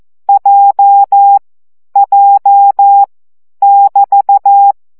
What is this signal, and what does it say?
Morse code, heard as a keyed tone, about 14 wpm, 800 Hz. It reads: J J <BT>
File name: CW gyakorlas j 12 baud